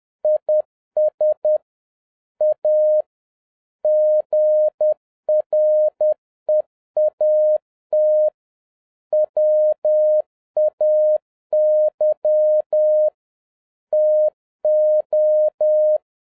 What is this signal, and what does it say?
Morse code, heard as a keyed tone, about 10 wpm, 610 Hz. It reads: IS A GREAT WAY TO
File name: 2011-07-mp3 / n21 GREETINGS FROM SINGAPORE 10wpm